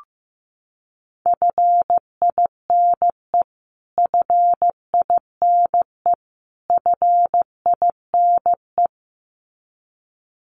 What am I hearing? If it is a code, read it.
Morse code, heard as a keyed tone, about 15 wpm, 700 Hz. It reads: FINE FINE FINE
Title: Top 500 Words Repeated 15wpm